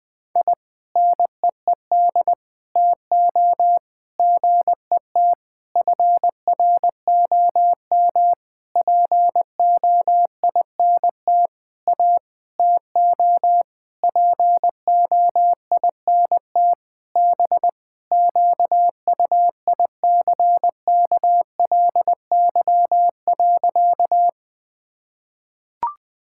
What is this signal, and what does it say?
Morse code, heard as a keyed tone, about 20 wpm, 700 Hz. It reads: I NEED TO GET FROM POINT A TO POINT B QUICKLY.
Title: Sentences with 7-Word Phrases - 20wpm